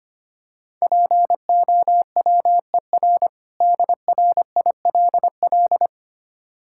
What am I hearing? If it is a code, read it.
Morse code, heard as a keyed tone, about 25 wpm, 700 Hz. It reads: POWER DRILL
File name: practice / 25wpm / Difficult Words - Speed-Racing with Hints - 25wpm